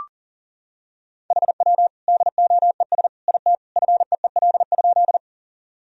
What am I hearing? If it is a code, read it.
Morse code, heard as a keyed tone, about 37 wpm, 700 Hz. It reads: HW DOES IT FEEL?
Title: Sentences from Top 400 Words and Abbreviations 40wpm